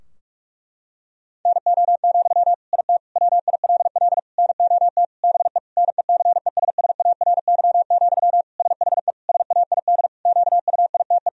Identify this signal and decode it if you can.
Morse code, heard as a keyed tone, about 47 wpm, 700 Hz. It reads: NO, IT WILL NOT BE NECESSARY, SHE SAID QUITE
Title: Sentences from Top 1000 Words - Review Part 4 - 45wpm